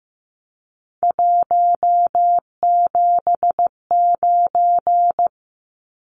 Morse code, heard as a keyed tone, about 15 wpm, 700 Hz. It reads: 179
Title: Numbers - 3-Digit 15wpm